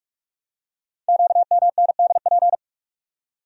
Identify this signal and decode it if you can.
Morse code, heard as a keyed tone, about 44 wpm, 700 Hz. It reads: QMNDP
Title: Code Groups - 5 Letters - 45wpm